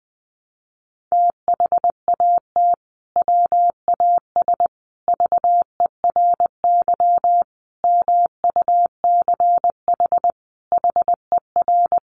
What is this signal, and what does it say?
Morse code, heard as a keyed tone, about 20 wpm, 700 Hz. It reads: THAT WAS VERY MUCH HER
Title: Sentences from Top 1200 Words 20wpm